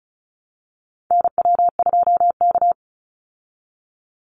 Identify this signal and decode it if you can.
Morse code, heard as a keyed tone, about 34 wpm, 700 Hz. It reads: NW2K
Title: Call Signs Repeated 3x - 35wpm